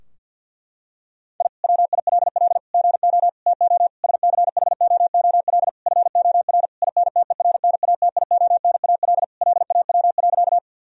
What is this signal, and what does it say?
Morse code, heard as a keyed tone, about 52 wpm, 700 Hz. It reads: I WILL GO TO SCHOOL FOR INTERNATIONAL LAW.
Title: Sentences from Top 1000 Words - Review Part 4 - 50wpm